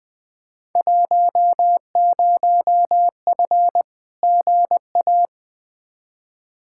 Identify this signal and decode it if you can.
Morse code, heard as a keyed tone, about 20 wpm, 700 Hz. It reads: 10F GA
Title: Field Day Exchanges Encore - 20wpm